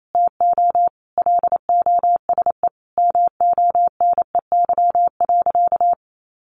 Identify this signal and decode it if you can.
Morse code, heard as a keyed tone, about 28 wpm, 700 Hz. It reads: TO LOSE MONEY.
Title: Sentences from Top 1100 Words 28wpm